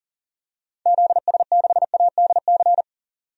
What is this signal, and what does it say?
Morse code, heard as a keyed tone, about 40 wpm, 700 Hz. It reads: ZS6ADC
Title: DXCC-100 Call Signs - 6 Characters - 40wpm